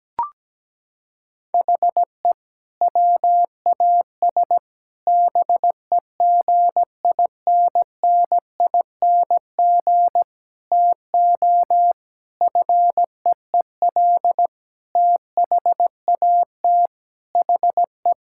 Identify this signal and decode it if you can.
Morse code, heard as a keyed tone, about 17 wpm, 700 Hz. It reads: HE WAS BEGINNING TO FEEL THAT HE
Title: Sentences from Top 1000 Words - Review Part 5 - 17wpm